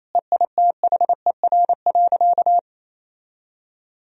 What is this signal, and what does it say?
Morse code, heard as a keyed tone, about 28 wpm, 700 Hz. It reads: EITHER.
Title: Sentences from Top 1000 Words - Review Part 4 - 28wpm